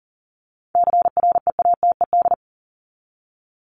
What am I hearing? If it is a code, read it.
Morse code, heard as a keyed tone, about 40 wpm, 700 Hz. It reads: CREATED
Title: ED Suffix - Top 100 Words with Base Words 40wpm